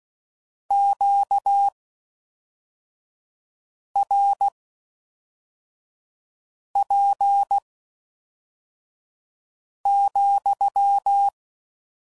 Morse code, heard as a keyed tone, about 16 wpm, spaced as for 4 wpm, 800 Hz. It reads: QRP,